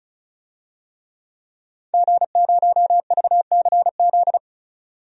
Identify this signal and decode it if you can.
Morse code, heard as a keyed tone, about 34 wpm, 700 Hz. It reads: G0VCZ
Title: DXCC-100 Call Signs - 5 Characters - 35wpm